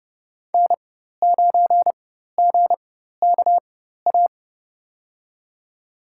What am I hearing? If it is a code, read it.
Morse code, heard as a keyed tone, about 30 wpm, spaced as for 14 wpm, 700 Hz. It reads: N9GKA
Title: USA Call Signs - 4x Character Spacing - 30wpm